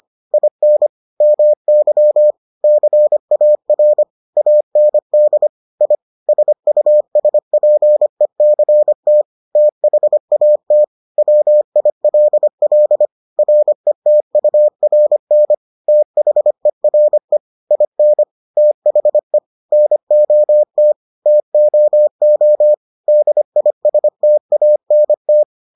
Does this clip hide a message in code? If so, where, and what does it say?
Yes: Morse code from 0.3 to 25.4 s, heard as a keyed tone, about 25 wpm, 600 Hz. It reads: IN MY CAR AND I SUSPECT THAT WILL RETURN THERE IN THE NOT TOO DISTANT